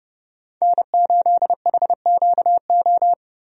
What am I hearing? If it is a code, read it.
Morse code, heard as a keyed tone, about 30 wpm, 700 Hz. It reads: N8HQO
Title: USA Call Signs - 5 Characters - Repeated 3x - 30wpm